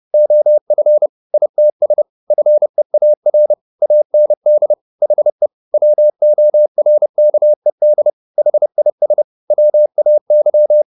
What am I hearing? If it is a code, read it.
Morse code, heard as a keyed tone, about 30 wpm, 600 Hz. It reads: OF ITS FEAR AND HE WORKED HIS WAY